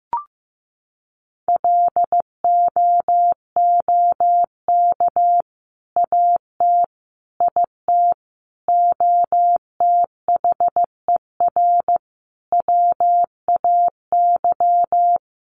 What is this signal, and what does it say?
Morse code, heard as a keyed tone, about 15 wpm, 700 Hz. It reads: LOOK AT IT OTHER WAY
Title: Sentences from Top 200 Words 15wpm